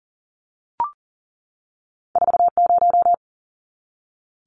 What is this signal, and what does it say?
Morse code, heard as a keyed tone, about 38 wpm, 700 Hz. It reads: 40
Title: Numbers - 2-Digit 40wpm